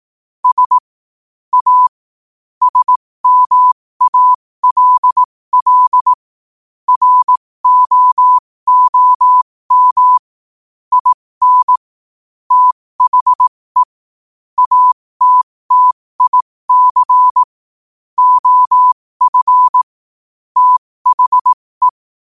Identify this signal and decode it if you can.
Morse code, heard as a keyed tone, about 18 wpm, spaced as for 15 wpm, 1000 Hz. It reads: S A SMALL ROOM IN THE ATTIC OF THE
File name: qotdfiles / 15wpm